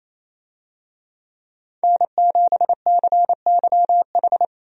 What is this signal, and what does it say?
Morse code, heard as a keyed tone, about 28 wpm, 700 Hz. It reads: N7CYH
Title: USA Call Signs - 5 Characters - 28wpm